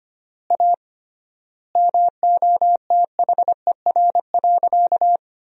Morse code, heard as a keyed tone, about 25 wpm, 700 Hz. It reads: A MOTHER.
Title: 3x Word-Spacing - Sentences from Top 200 Words - 25wpm